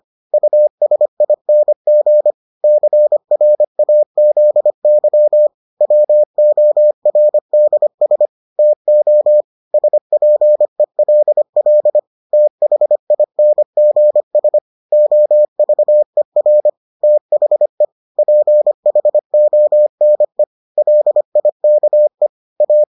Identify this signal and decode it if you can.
Morse code, heard as a keyed tone, about 25 wpm, 600 Hz. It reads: USING CRAZY WORDS TO SPELL THINGS OVER THE PHONE LIKE A